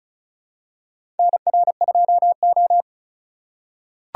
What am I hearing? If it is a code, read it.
Morse code, heard as a keyed tone, about 36 wpm, 700 Hz. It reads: NR2O
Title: USA Call Signs - 4 Characters - 4x Character Spacing - 35wpm